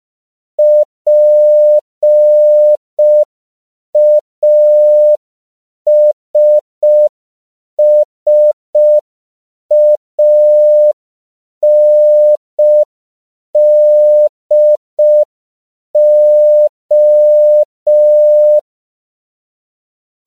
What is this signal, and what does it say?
Morse code, heard as a keyed tone, about 5 wpm, 600 Hz. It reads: PASSANDO
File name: CW / 5PPM